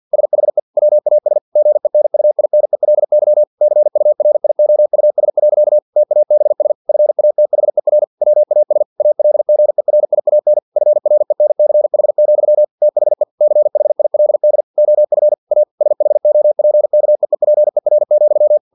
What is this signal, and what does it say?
Morse code, heard as a keyed tone, about 50 wpm, 600 Hz. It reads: SHE WAS GENUINELY CURIOUS. TABS FATHER WAS ALGERIAN FRENCH, THE CHILD OF A SHOPKEEPER,
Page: Never 49wpm